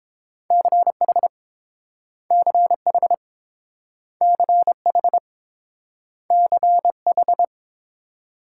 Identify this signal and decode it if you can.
Morse code, heard as a keyed tone, about 30 wpm, 690 Hz. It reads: CH CH CH CH